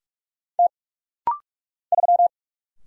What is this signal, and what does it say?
Morse code, heard as a keyed tone, about 44 wpm, 700 Hz. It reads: T 3
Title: Single Letter-Number Alt - Rapid-Fire 45wpm